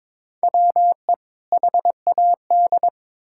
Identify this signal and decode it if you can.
Morse code, heard as a keyed tone, about 22 wpm, 700 Hz. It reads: WE HAD
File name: Build A Phrase - 2-Word Phrases - 22wpm